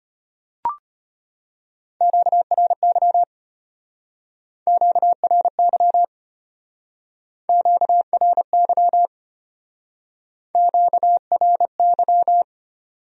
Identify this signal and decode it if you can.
Morse code, heard as a keyed tone, about 33 wpm, 700 Hz. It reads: QRY QRY QRY QRY